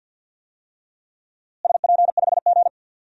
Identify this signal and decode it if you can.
Morse code, heard as a keyed tone, about 50 wpm, 700 Hz. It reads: SP5G
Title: DXCC-100 Call Signs - 4 Characters - Repeated 3x - 50wpm